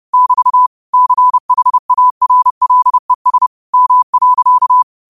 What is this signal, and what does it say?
Morse code, heard as a keyed tone, about 30 wpm, 1000 Hz. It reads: <BT> CHARLES M.